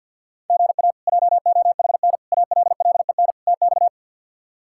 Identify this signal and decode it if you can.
Morse code, heard as a keyed tone, about 51 wpm, 700 Hz. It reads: GA JOHN ALLEN TX
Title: K1USN SST QSO Practice - 50wpm